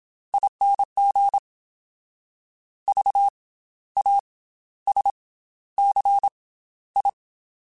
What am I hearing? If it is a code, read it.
Morse code, heard as a keyed tone, about 27 wpm, spaced as for 13 wpm, 800 Hz. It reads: ING VASCI
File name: Special INGs / Special INGs-mp3s / Slow ing @25